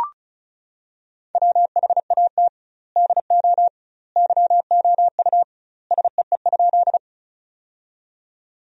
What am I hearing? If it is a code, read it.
Morse code, heard as a keyed tone, about 36 wpm, 700 Hz. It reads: WHAT DO YOU SEE?
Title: 2x Word-Spacing - Sentences from Top 100 Words - 35wpm